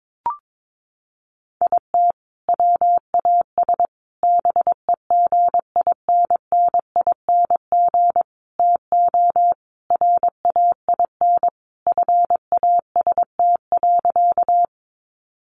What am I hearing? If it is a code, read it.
Morse code, heard as a keyed tone, about 22 wpm, 700 Hz. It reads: IT WAS BEGINNING TO RAIN FAST.
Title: Sentences from Top 800 Words 22wpm